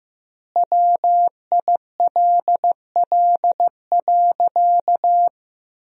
Morse code, heard as a keyed tone, about 15 wpm, 700 Hz. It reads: WILL.
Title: 2x Word-Spacing - Sentences from Top 500 Words - 15wpm